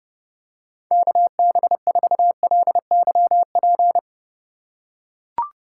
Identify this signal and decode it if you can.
Morse code, heard as a keyed tone, about 29 wpm, 700 Hz. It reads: KB4LYP